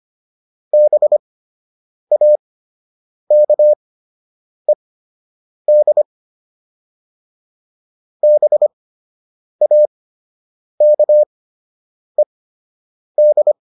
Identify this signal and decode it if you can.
Morse code, heard as a keyed tone, about 25 wpm, spaced as for 8 wpm, 600 Hz. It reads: BAKED BAKED